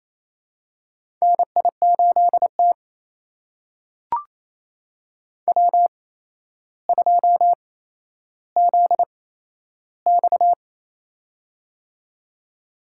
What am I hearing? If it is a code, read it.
Morse code, heard as a keyed tone, about 27 wpm, spaced as for 12 wpm, 700 Hz. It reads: NI8T W2ZX
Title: USA Call Signs - 4 Characters - 8x Character Spacing - 28wpm